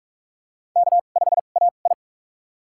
Morse code, heard as a keyed tone, about 44 wpm, spaced as for 32 wpm, 690 Hz. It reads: K5AI
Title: USA Call Signs - 4 Characters - 2x Character Spacing - 45wpm